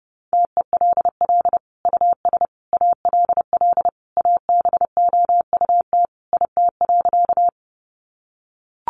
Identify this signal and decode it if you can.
Morse code, heard as a keyed tone, about 30 wpm, 700 Hz. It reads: TELL US ALL ABOUT IT.
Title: Sentences from Top 200 Words 30wpm